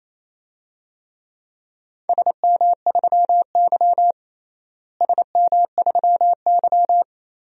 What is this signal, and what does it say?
Morse code, heard as a keyed tone, about 28 wpm, 700 Hz. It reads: SM3Y SM3Y
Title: DXCC-100 Call Signs - 4 Characters - Repeated 2x - 28wpm